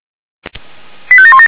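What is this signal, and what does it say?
Five-tone selective call (ZVEI), heard as quick runs of five tones, tones of about 70 ms. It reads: 85717
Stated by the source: ZVEI five-tone selective call 85752 received on SDR.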